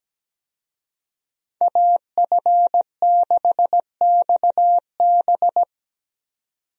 Morse code, heard as a keyed tone, about 17 wpm, 700 Hz. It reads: AF6XB
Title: USA Call Signs - 5 Characters - 17wpm